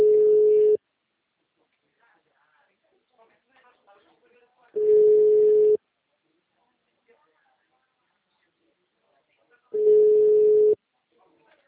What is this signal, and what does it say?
Ringback tone (Europe, 425 Hz), with speech.